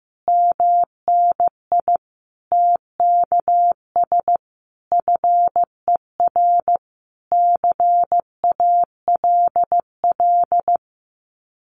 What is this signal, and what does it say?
Morse code, heard as a keyed tone, about 15 wpm, 700 Hz. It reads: MNI TKS FER CALL
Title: Common QSO Phrases 15wpm